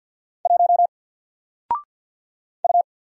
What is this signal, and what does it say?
Morse code, heard as a keyed tone, about 50 wpm, 700 Hz. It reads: 1 U
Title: Single Letter-Number Alt 50wpm